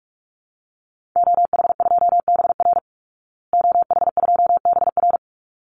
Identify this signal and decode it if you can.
Morse code, heard as a keyed tone, about 44 wpm, 700 Hz. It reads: OH2BR OH2BR
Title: Call Signs Repeated 3x - 45wpm